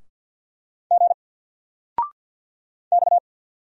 Morse code, heard as a keyed tone, about 50 wpm, 700 Hz. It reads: G X
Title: Single Character (Letter:Number:Punctuation:Prosign) 50wpm